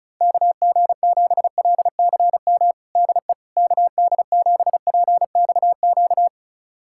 Morse code, heard as a keyed tone, about 34 wpm, 700 Hz. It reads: KG7LCM DE KD7PXQ
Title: USA Callsign DE Callsign - 6 Characters - 35wpm